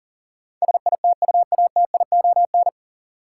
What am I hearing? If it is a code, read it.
Morse code, heard as a keyed tone, about 40 wpm, 700 Hz. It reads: SITUATION
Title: Sentences from Top 1200 Words - New Word Drill - 40wpm